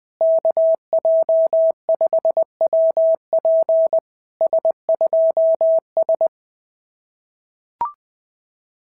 Morse code, heard as a keyed tone, about 19 wpm, 650 Hz. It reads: KJ5WP S2S
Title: SOTA QSO Practice 20wpm